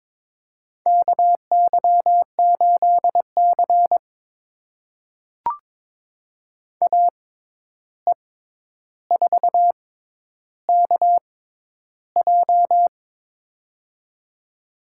Morse code, heard as a keyed tone, about 22 wpm, spaced as for 11 wpm, 700 Hz. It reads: KY8C AE4KJ
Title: USA Call Signs - 6x Character Spacing - 22wpm